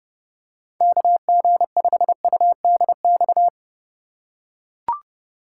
Morse code, heard as a keyed tone, about 31 wpm, 700 Hz. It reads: KG5UDX
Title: USA Call Signs - 4x Character Spacing - 30wpm